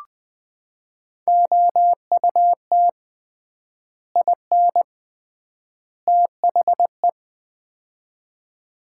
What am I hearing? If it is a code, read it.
Morse code, heard as a keyed tone, about 20 wpm, 700 Hz. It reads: OUT IN THE
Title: 3x Word-Spacing - Sets of 3 Words - 20wpm